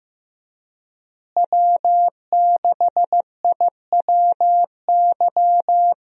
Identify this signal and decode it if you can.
Morse code, heard as a keyed tone, about 15 wpm, 700 Hz. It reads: W6IWY
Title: USA Call Signs - 5 Characters - Repeated 2x - 15wpm